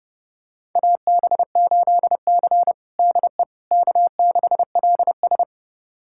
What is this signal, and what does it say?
Morse code, heard as a keyed tone, about 30 wpm, 700 Hz. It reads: AB8C DE K6LS
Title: USA Callsign DE Callsign - 4 Characters - 30wpm